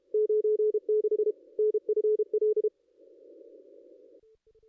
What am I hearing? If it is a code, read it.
Morse code, heard as a keyed tone, about 32 wpm, 440 Hz. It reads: 96 NFL
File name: contests / ss-cw-k5tr-op-n5rz -0500